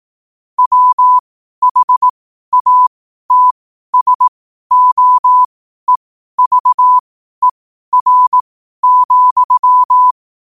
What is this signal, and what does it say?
Morse code, heard as a keyed tone, about 18 wpm, spaced as for 12 wpm, 1000 Hz. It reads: WHATSOEVER,
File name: qotdfiles / 12wpm